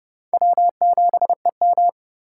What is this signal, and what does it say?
Morse code, heard as a keyed tone, about 32 wpm, 700 Hz. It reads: W7EM
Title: USA Call Signs - 4 Characters - 4x Character Spacing - 30wpm